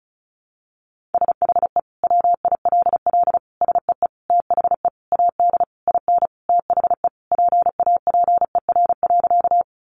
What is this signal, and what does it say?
Morse code, heard as a keyed tone, about 34 wpm, 700 Hz. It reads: SHE WILL SEE THE AD IN THE PAPER.
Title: Sentences from Top 1300 Words 35wpm